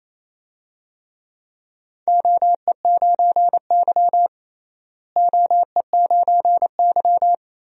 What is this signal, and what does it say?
Morse code, heard as a keyed tone, about 28 wpm, 700 Hz. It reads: OE9Y OE9Y
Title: DXCC-100 Call Signs - 4 Characters - Repeated 2x - 28wpm